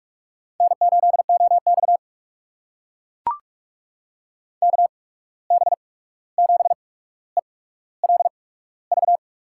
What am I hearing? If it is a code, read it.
Morse code, heard as a keyed tone, about 44 wpm, spaced as for 17 wpm, 700 Hz. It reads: N8OX KB7ELV
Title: USA Call Signs - 8x Character Spacing - 45wpm